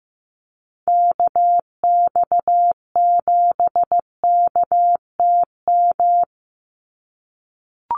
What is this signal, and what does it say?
Morse code, heard as a keyed tone, about 15 wpm, 700 Hz. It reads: KX7KTM
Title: USA Call Signs 15wpm